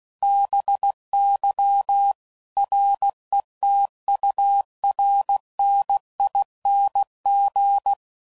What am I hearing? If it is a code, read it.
Morse code, heard as a keyed tone, about 16 wpm, 800 Hz. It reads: BY RETURNING
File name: CW Code Groups @15WPM / Technical Subjects / XC